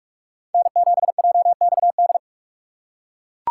Morse code, heard as a keyed tone, about 40 wpm, 700 Hz. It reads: N7JXD